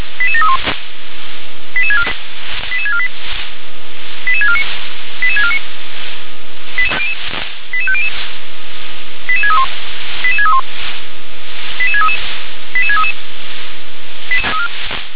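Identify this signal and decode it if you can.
Five-tone selective call (ZVEI), heard as quick runs of five tones, tones of about 70 ms. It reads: 88731 88648 88649 88649 88500 88631 88631 88633 88633
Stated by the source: ZVEI five-tone selective call received on an SDR.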